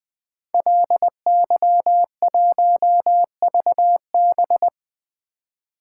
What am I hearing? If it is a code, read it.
Morse code, heard as a keyed tone, about 20 wpm, 700 Hz. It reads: LY1VB
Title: Code Groups - 5 Letter:Numbers - 20wpm